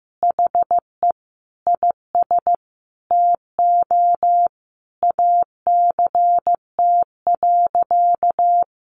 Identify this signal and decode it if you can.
Morse code, heard as a keyed tone, about 15 wpm, 700 Hz. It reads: HE IS TO ACT.
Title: Sentences from Top 1200 Words 15wpm